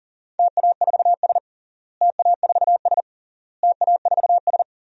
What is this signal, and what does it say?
Morse code, heard as a keyed tone, about 40 wpm, 700 Hz. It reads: TA4S TA4S TA4S
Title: DXCC-100 Call Signs - 4 Characters - Repeated 3x - 40wpm